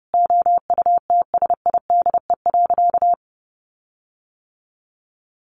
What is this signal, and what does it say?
Morse code, heard as a keyed tone, about 30 wpm, 700 Hz. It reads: OUTSIDE.
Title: Sentences from Top 1300 Words 30wpm